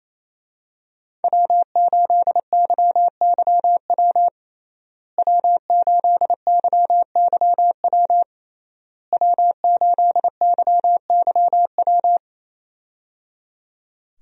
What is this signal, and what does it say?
Morse code, heard as a keyed tone, about 28 wpm, 700 Hz. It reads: W8YYW W8YYW W8YYW
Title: USA Call Signs - 5 Characters - Repeated 3x - 28wpm